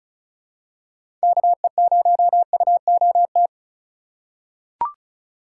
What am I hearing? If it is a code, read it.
Morse code, heard as a keyed tone, about 36 wpm, 700 Hz. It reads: KE0UOT
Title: USA Call Signs - 6 Characters - 4x Character Spacing - 35wpm